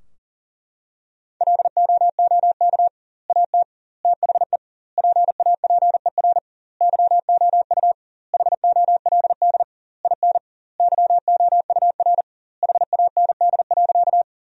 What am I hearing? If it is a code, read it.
Morse code, heard as a keyed tone, about 40 wpm, 700 Hz. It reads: LOOK AT THE PAPER YOU HOLD IN YOUR HAND.
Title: 2x Word-Spacing - Sentences from Top 400 Words - 40wpm